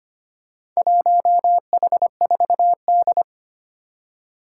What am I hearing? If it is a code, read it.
Morse code, heard as a keyed tone, about 25 wpm, 700 Hz. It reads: 1H4D
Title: Four Challenging Letter-Number Combinations 25wpm